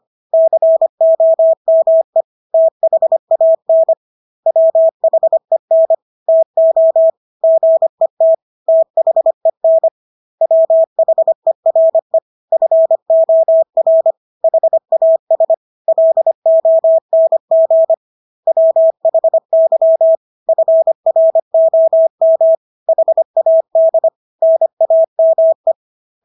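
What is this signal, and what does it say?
Morse code, heard as a keyed tone, about 25 wpm, 650 Hz. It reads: COME THAN WHEN TO GET THEN WHERE FOR HAS LONG WHY FROM HAD NAME
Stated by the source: Morse practice of English words at 25 wpm from CW med Gnister.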